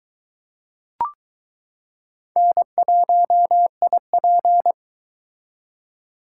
Morse code, heard as a keyed tone, about 20 wpm, 700 Hz. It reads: N1IP